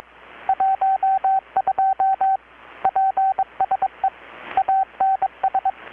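Morse code, heard as a keyed tone, about 22 wpm, 770 Hz, with radio noise. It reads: 12 PSE ANS
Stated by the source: Morse from KPH's Night of Nights XX maritime radiotelegraphy event, received in Chiapas, Mexico about 3,800 km away on a Kenwood R-600 with a grounded 10 m long-wire antenna.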